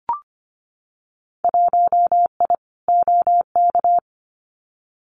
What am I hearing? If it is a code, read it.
Morse code, heard as a keyed tone, about 25 wpm, 700 Hz. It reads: 1I OK
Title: Winter Field Day Exchanges Encore 25wpm